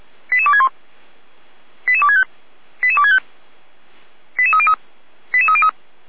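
Five-tone selective call (ZVEI), heard as quick runs of five tones, tones of about 70 ms. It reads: 88261 88265 88265 88303 88303
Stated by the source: ZVEI five-tone selective call received on an SDR.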